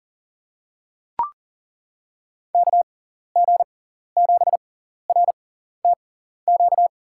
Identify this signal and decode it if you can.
Morse code, heard as a keyed tone, about 40 wpm, spaced as for 14 wpm, 700 Hz. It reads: KG7RTQ
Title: USA Call Signs - 6x Character Spacing - 40wpm